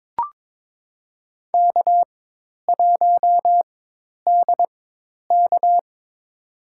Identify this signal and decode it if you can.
Morse code, heard as a keyed tone, about 22 wpm, spaced as for 10 wpm, 700 Hz. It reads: K1DK
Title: USA Call Signs - 4 Characters - 4x Character Spacing - 22wpm